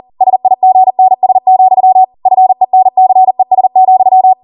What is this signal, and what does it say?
Morse code, heard as a keyed tone, about 40 wpm, 750 Hz. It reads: SIGNS, FENCES,